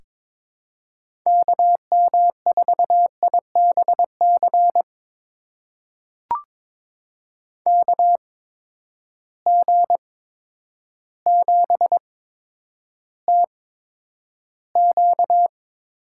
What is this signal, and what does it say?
Morse code, heard as a keyed tone, about 22 wpm, spaced as for 10 wpm, 700 Hz. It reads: KM4IBC KG7TQ
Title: USA Call Signs - 6 Characters - 8x Character Spacing - 22wpm